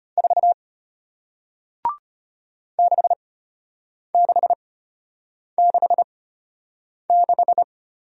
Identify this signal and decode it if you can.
Morse code, heard as a keyed tone, about 34 wpm, spaced as for 8 wpm, 700 Hz. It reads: 4 6666